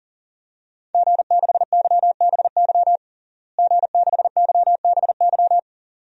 Morse code, heard as a keyed tone, about 40 wpm, 700 Hz. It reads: G6YBY G6YBY
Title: DXCC-100 Call Signs - 5 Characters - Repeated 2x - 40wpm